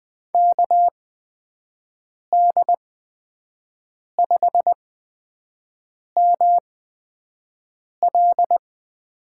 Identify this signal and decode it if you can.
Morse code, heard as a keyed tone, about 20 wpm, spaced as for 5 wpm, 700 Hz. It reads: KD5ML
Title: USA Call Signs - 6 Characters - 8x Character Spacing - 20wpm